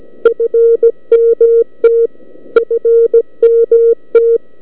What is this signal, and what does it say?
Morse code, heard as a keyed tone, about 17 wpm, 460 Hz. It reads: FMT FMT